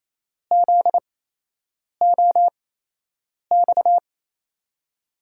Morse code, heard as a keyed tone, about 28 wpm, spaced as for 8 wpm, 700 Hz. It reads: ZOX
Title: USA Call Signs - 6 Characters - 8x Character Spacing - 28wpm